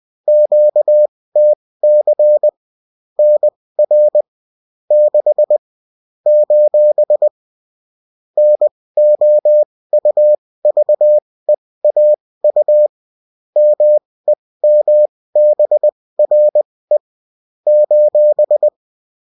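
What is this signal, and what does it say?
Morse code, heard as a keyed tone, about 20 wpm, spaced as for 16 wpm, 600 Hz. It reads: QTC NR 6 : NOUVEAU MEMBRE :
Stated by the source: Bulletin in text and CW form.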